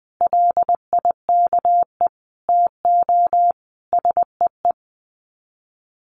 Morse code, heard as a keyed tone, about 20 wpm, 700 Hz. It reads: LIKE TO SEE
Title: Sets of 5 Words Encore2 20wpm